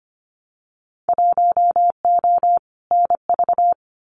Morse code, heard as a keyed tone, about 25 wpm, 700 Hz. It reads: 1O NV
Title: Winter Field Day Exchanges Encore 25wpm